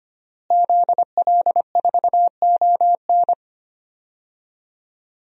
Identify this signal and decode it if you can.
Morse code, heard as a keyed tone, about 25 wpm, 700 Hz. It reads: ZL4ON
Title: DXCC-100 Call Signs - 5 Characters - Repeated 3x - 25wpm